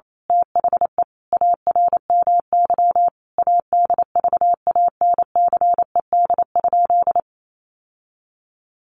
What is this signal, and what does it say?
Morse code, heard as a keyed tone, about 28 wpm, 700 Hz. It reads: THE ARMY ADVANCED?
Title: Sentences from Top 1200 Words 28wpm